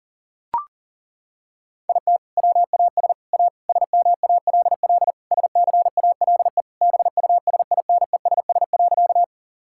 Morse code, heard as a keyed tone, about 40 wpm, 700 Hz. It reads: IT WAS A SMALL SCALE BUSINESS.
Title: Sentences from Top 1000 Words - Review Part 5 - 40wpm